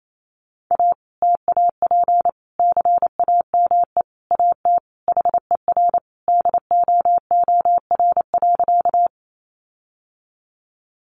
Morse code, heard as a keyed tone, about 28 wpm, 700 Hz. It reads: A TAP CAME AT HER DOOR.
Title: Sentences from Top 1400 Words 28wpm